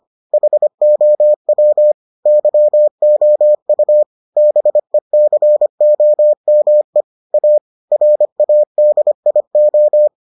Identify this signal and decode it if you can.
Morse code, heard as a keyed tone, about 25 wpm, 600 Hz. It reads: HOW YOU BECOME A RADIO